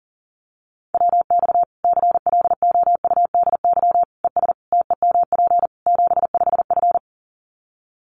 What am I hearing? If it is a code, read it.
Morse code, heard as a keyed tone, about 40 wpm, 700 Hz. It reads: WX CLOUDY ES TEMP 75F
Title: Common QSO Phrases 40wpm